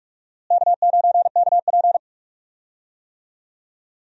Morse code, heard as a keyed tone, about 44 wpm, 700 Hz. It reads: K9KP
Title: USA Call Signs - 4 Characters - Repeated 2x - 45wpm